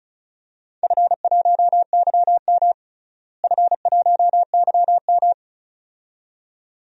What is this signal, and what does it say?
Morse code, heard as a keyed tone, about 36 wpm, 700 Hz. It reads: F1YM F1YM
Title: DXCC-100 Call Signs - 4 Characters - Repeated 2x - 35wpm